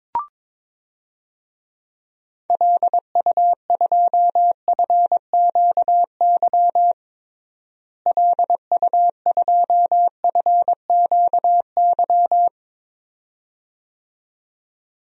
Morse code, heard as a keyed tone, about 22 wpm, 700 Hz. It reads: LU2FQY LU2FQY
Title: DXCC-100 Call Signs - 6 Characters - Repeated 2x - 22wpm